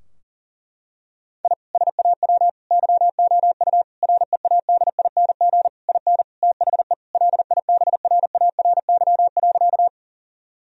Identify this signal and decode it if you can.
Morse code, heard as a keyed tone, about 40 wpm, 700 Hz. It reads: I SAW YOU READING IN THE LIBRARY.
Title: Sentences from Top 1000 Words - Review Part 4 - 40wpm